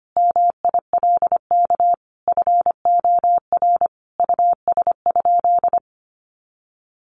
Morse code, heard as a keyed tone, about 25 wpm, 700 Hz. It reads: MILK FOR US?
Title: Sentences from Top 1500 Words 25wpm